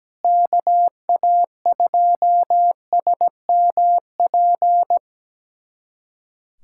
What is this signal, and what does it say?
Morse code, heard as a keyed tone, about 17 wpm, 700 Hz. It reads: KA2SMP
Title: USA Callsign DE Callsign - 6 Characters - 17wpm